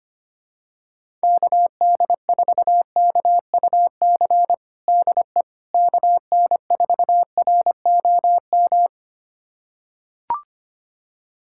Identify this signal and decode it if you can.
Morse code, heard as a keyed tone, about 25 wpm, 700 Hz. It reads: KD4KUC DE KN4ROM
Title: USA Callsign DE Callsign 25wpm